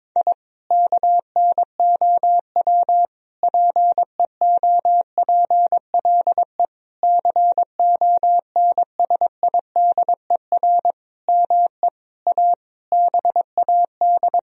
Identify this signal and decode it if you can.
Morse code, heard as a keyed tone, about 22 wpm, 700 Hz. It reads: I KNOW PEOPLE CONSIDER ME A BAD
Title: Sentences from Top 1000 Words - Review Part 4 - 22wpm